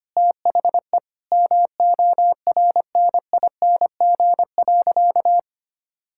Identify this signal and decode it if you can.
Morse code, heard as a keyed tone, about 25 wpm, 700 Hz. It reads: THE MORNING.
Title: Sentences from Top 1000 Words - Review Part 5 - 25wpm